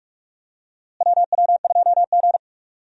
Morse code, heard as a keyed tone, about 42 wpm, 700 Hz. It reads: WW2G
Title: USA Call Signs - 4 Characters - Repeated 2x - 45wpm